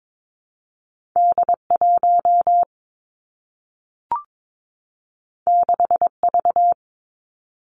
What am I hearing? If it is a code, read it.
Morse code, heard as a keyed tone, about 22 wpm, 700 Hz. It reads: D 1 6 V
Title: Two Challenging Letter-Number Combinations 22wpm